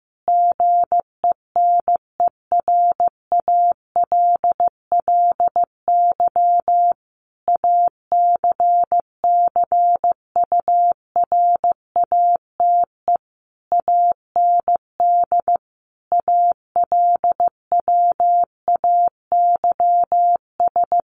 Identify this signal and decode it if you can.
Morse code, heard as a keyed tone, about 14 wpm, 700 Hz. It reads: GENERALLY ACCURATE AND ALWAYS
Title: Sentences from Top 1200 Words 15wpm